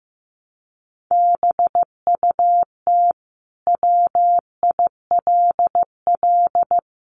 Morse code, heard as a keyed tone, about 15 wpm, 700 Hz. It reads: BUT WILL
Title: Sentences from Top 1400 Words 15wpm